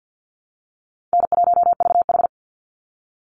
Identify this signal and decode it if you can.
Morse code, heard as a keyed tone, about 48 wpm, 700 Hz. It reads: N1VH
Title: Call Signs Repeated 3x - 50wpm